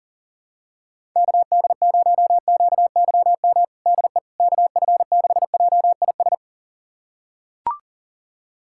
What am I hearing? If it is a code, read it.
Morse code, heard as a keyed tone, about 40 wpm, 690 Hz. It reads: KD0QYM DE KF6JIS
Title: USA Callsign DE Callsign - 6 Characters - 40wpm